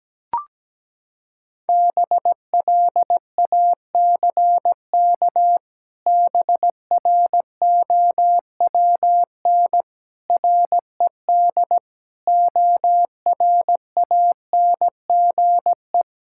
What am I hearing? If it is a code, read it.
Morse code, heard as a keyed tone, about 17 wpm, 700 Hz. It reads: BLACK BROWN RED ORANGE
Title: Pangram Challenge v2 - 17wpm